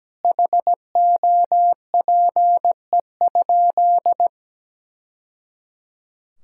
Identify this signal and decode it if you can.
Morse code, heard as a keyed tone, about 17 wpm, 700 Hz. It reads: HOPE?
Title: Sentences from Top 1000 Words - Review Part 5 - 17wpm